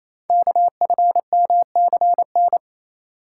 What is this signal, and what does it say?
Morse code, heard as a keyed tone, about 27 wpm, 700 Hz. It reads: KFMCN
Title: Code Groups - 5 Letters - 28wpm